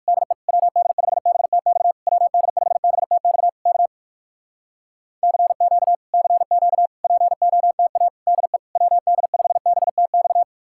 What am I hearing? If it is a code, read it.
Morse code, heard as a keyed tone, about 66 wpm, spaced as for 57 wpm, 700 Hz. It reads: DE WD5BTX WD5BTX K CQ CQ POTA DE WD5BTX